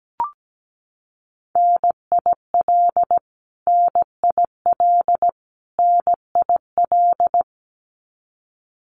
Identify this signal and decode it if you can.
Morse code, heard as a keyed tone, about 17 wpm, 700 Hz. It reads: NIL NIL NIL
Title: Abbreviations Repeated 3x - 17wpm